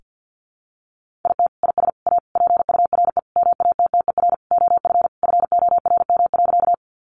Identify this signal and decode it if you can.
Morse code, heard as a keyed tone, about 50 wpm, 700 Hz. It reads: IT IS A PURE MATTER OF FORM.
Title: Sentences from Top 1500 Words 50wpm